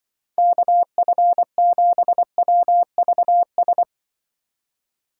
Morse code, heard as a keyed tone, about 25 wpm, 700 Hz. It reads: KF7WVS
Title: Speed Racing - USA 2x3 Call Signs 20wpm